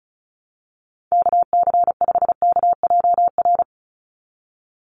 Morse code, heard as a keyed tone, about 34 wpm, 700 Hz. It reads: KC5KJR
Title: USA Call Signs 35wpm